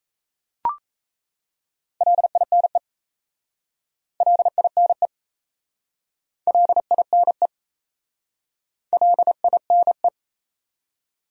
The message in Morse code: LINE LINE LINE LINE

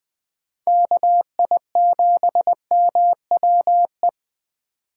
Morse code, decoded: KI7MWE

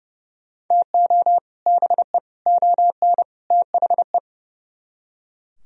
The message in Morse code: TO BE ON THE

